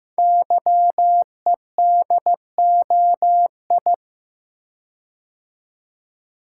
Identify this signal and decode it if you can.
Morse code, heard as a keyed tone, about 15 wpm, 700 Hz. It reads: YEDOI
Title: Code Groups - 5 Letters - 15wpm